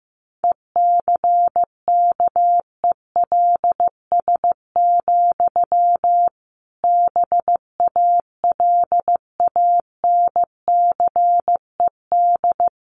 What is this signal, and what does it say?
Morse code, heard as a keyed tone, about 14 wpm, 700 Hz. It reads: ECKELS, BALANCED
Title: A Sound Of Thunder 15wpm